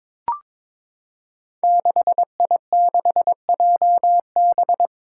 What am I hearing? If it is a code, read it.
Morse code, heard as a keyed tone, about 22 wpm, 700 Hz. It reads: 6I6JB
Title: Five Challenging Letter-Number Combinations 22wpm